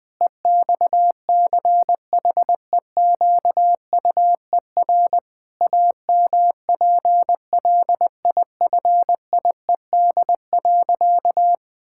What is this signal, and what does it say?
Morse code, heard as a keyed tone, about 20 wpm, 700 Hz. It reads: EXCHEQUER AMPLIFIED.
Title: Pangram Challenge v2 - 20wpm